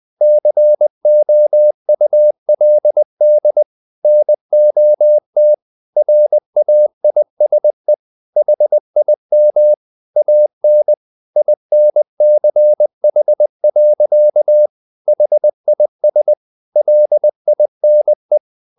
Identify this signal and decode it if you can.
Morse code, heard as a keyed tone, about 20 wpm, 600 Hz. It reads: COULD NOT RAISE HIM AN INCH. HIS LINE
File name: the old man and the sea / the old man and the sea part 1 20wpm